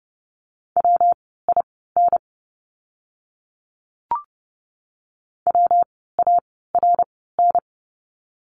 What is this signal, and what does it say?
Morse code, heard as a keyed tone, about 30 wpm, spaced as for 17 wpm, 700 Hz. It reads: WIN WARN
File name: course / 30:10wpm / 115 Lesson 19f - Letter w - Words 30wpm